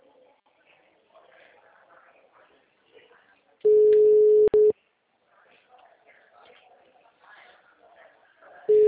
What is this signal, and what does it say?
Ringback tone (Europe, 425 Hz), with speech.